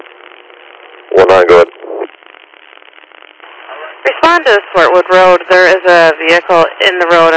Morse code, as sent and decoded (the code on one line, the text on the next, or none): none